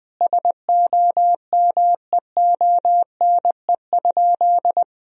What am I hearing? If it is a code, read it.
Morse code, heard as a keyed tone, about 20 wpm, 700 Hz. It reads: SOMEONE?
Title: Sentences from Top 1000 Words - Review Part 4 - 20wpm